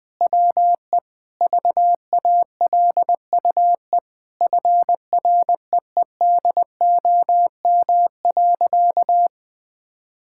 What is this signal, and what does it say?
Morse code, heard as a keyed tone, about 20 wpm, 700 Hz. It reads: WE VALUE FREEDOM.
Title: Sentences with 3-Word Phrases - 20wpm